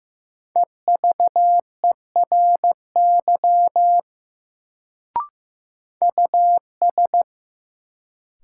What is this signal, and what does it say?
Morse code, heard as a keyed tone, about 15 wpm, 700 Hz. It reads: EVERY US
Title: Sentences from Top 200 Words - New Word Drill - 15wpm